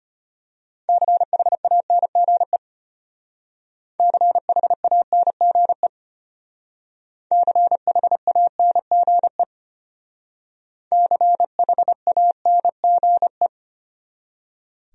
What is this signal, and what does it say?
Morse code, heard as a keyed tone, about 33 wpm, 700 Hz. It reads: CHANGE CHANGE CHANGE CHANGE